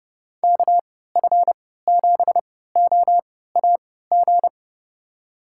Morse code, heard as a keyed tone, about 30 wpm, spaced as for 17 wpm, 700 Hz. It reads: KF7OAG
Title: USA Call Signs - 6 Characters - 3x Character Spacing - 30wpm